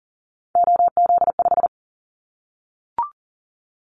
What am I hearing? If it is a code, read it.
Morse code, heard as a keyed tone, about 42 wpm, 690 Hz. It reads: OZ5
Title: Callsign Prefixes for Top 100 DXCC - 40wpm